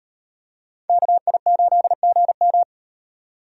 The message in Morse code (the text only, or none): KI8GM